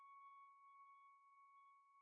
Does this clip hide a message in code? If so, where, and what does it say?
No.